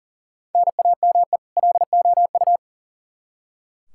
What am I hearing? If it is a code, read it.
Morse code, heard as a keyed tone, about 40 wpm, 700 Hz. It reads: NAME LOU
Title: Practice QSO - Exchange 1 - Beginning - 40wpm